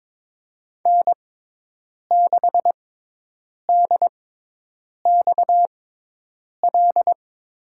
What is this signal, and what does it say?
Morse code, heard as a keyed tone, about 22 wpm, spaced as for 8 wpm, 700 Hz. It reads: N6DXL
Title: USA Call Signs - 6x Character Spacing - 22wpm